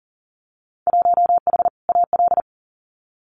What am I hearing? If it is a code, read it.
Morse code, heard as a keyed tone, about 40 wpm, 700 Hz. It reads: 1H AL
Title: Winter Field Day Exchanges 40wpm